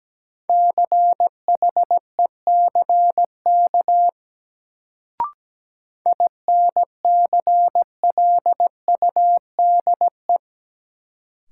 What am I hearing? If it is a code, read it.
Morse code, heard as a keyed tone, about 17 wpm, 700 Hz. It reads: CHECK INCLUDE
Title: Sentences from Top 1000 Words - Review Part 3 - New Word Drill - 17wpm